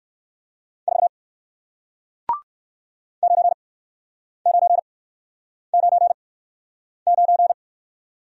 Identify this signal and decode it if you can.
Morse code, heard as a keyed tone, about 61 wpm, spaced as for 9 wpm, 700 Hz. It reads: 4 9999